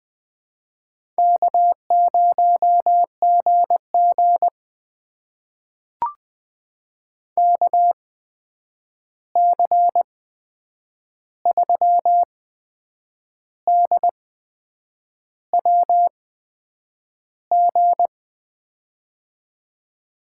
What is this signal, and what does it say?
Morse code, heard as a keyed tone, about 20 wpm, spaced as for 8 wpm, 700 Hz. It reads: K0GG KC3DWG
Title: USA Call Signs - 8x Character Spacing - 20wpm